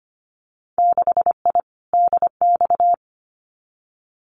Morse code, heard as a keyed tone, about 25 wpm, 700 Hz. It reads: 6I DX
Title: Winter Field Day Exchanges Encore 25wpm